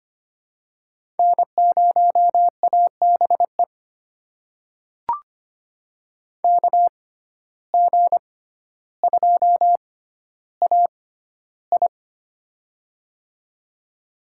Morse code, heard as a keyed tone, about 25 wpm, spaced as for 13 wpm, 700 Hz. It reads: N0ABE KG2AI
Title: USA Call Signs - 5 Characters - 6x Character Spacing - 25wpm